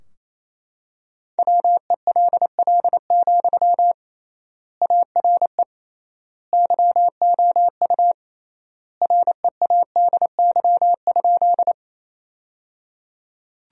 Morse code, heard as a keyed tone, about 28 wpm, 700 Hz. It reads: WELL, ARE YOU READY?
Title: 3x Word-Spacing - Sentences from Top 400 Words - 28wpm